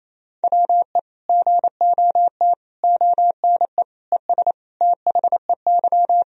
Morse code, heard as a keyed tone, about 28 wpm, 700 Hz. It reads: WE GOT ONE ES THEY
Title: Sentences from Top 300 Words and Abbreviations 28wpm